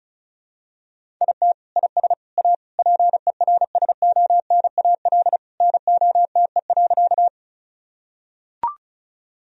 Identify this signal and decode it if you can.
Morse code, heard as a keyed tone, about 36 wpm, 700 Hz. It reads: IT IS A PERSONAL NOTE.
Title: Sentences from Top 1000 Words - Review Part 4 - 35wpm